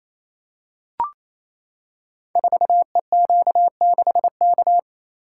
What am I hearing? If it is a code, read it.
Morse code, heard as a keyed tone, about 28 wpm, 700 Hz. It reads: 4EQ6K
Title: Code Groups - 5 Letter:Numbers - 28wpm